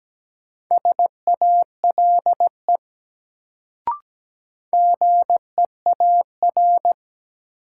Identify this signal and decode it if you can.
Morse code, heard as a keyed tone, about 17 wpm, 700 Hz. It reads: SALE GEAR
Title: Sentences from Top 1400 Words - New Word Drill - 17wpm